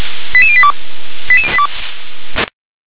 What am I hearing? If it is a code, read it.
Five-tone selective call (ZVEI), heard as quick runs of five tones, tones of about 70 ms. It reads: 88082